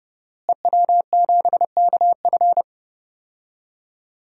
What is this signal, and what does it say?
Morse code, heard as a keyed tone, about 30 wpm, 700 Hz. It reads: EW7KF
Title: DXCC-100 Call Signs - 5 Characters - Repeated 2x - 30wpm